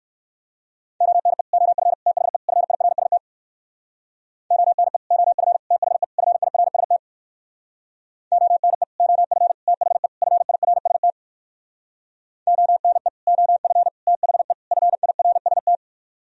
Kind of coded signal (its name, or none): Morse code